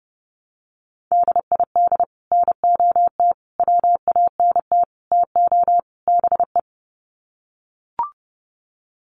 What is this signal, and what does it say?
Morse code, heard as a keyed tone, about 30 wpm, 700 Hz. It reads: DID NOT WANT TO BE
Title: Sets of 5 Words 30wpm